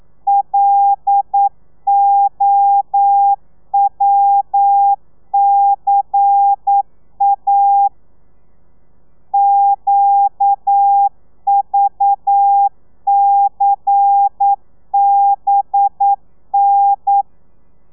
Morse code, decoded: LOWCA QVCBN